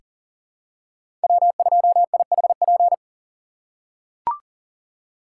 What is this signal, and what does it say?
Morse code, heard as a keyed tone, about 40 wpm, 700 Hz. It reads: W2IHP
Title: USA Call Signs - 3x Character Spacing - 40wpm